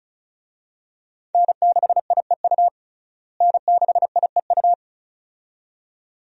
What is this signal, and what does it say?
Morse code, heard as a keyed tone, about 34 wpm, 700 Hz. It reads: N6IEU N6IEU
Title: USA Call Signs - 5 Characters - Repeated 2x - 35wpm